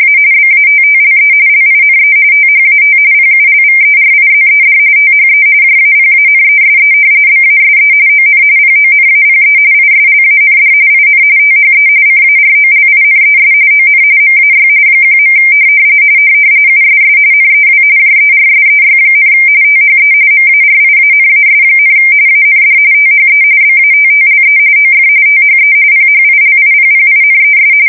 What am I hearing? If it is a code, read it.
RTTY, about 45.45 baud. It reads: A BATTLE LINE THAT PULVERIZED THE JAPANESE BASE ON PONAPE ISLAND IN THE CAROLINES ON MAY 1. USS NEW JERSEY, THE MARIANAS, AND THE BATTLE OF THE PHILIPPINES
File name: fsk170 / battleshipnj